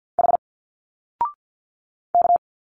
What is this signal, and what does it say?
Morse code, heard as a keyed tone, about 50 wpm, 700 Hz. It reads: H K
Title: Single Letters 50wpm